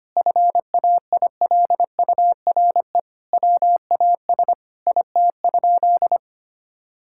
Morse code, heard as a keyed tone, about 25 wpm, 700 Hz. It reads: FAILURE WAS IT?
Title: Sentences from Top 1000 Words - Review Part 6 - 25wpm